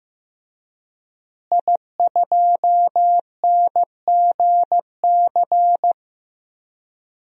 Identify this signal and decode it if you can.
Morse code, heard as a keyed tone, about 15 wpm, 700 Hz. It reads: I2NGC
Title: DXCC-100 Call Signs - 5 Characters - Repeated 3x - 15wpm